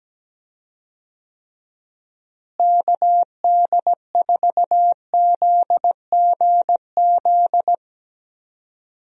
Morse code, heard as a keyed tone, about 17 wpm, 700 Hz. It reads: KD4ZGZ
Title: USA Call Signs - 6 Characters - Repeated 3x - 17wpm